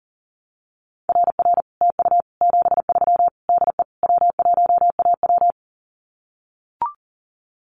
Morse code, heard as a keyed tone, about 40 wpm, 700 Hz. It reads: RR TU 73 DE W1AW
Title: Common QSO Phrases Encore 40wpm